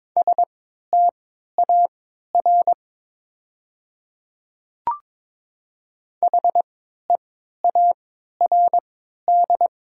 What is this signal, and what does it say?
Morse code, heard as a keyed tone, about 23 wpm, spaced as for 13 wpm, 690 Hz. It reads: STAR HEARD
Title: 3x Character Spacing - Top 500 Words - 22wpm